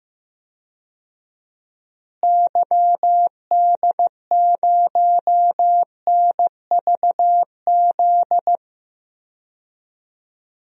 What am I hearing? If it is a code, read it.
Morse code, heard as a keyed tone, about 15 wpm, 700 Hz. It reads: YD0NVZ
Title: DXCC-100 Call Signs - 6 Characters - Repeated 3x - 15wpm